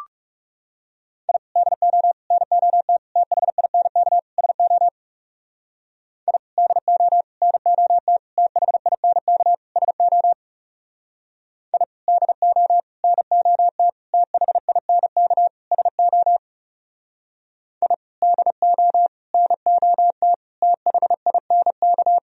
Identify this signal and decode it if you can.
Morse code, heard as a keyed tone, about 36 wpm, 700 Hz. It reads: I DO NOT THINK SO I DO NOT THINK SO I DO NOT THINK SO I DO NOT THINK
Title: Speed-Racing - Sets of 5 Words 30wpm